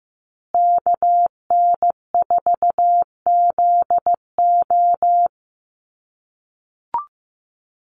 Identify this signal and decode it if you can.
Morse code, heard as a keyed tone, about 15 wpm, 700 Hz. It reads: KN4ZO
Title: USA Call Signs Repeated 2x 15wpm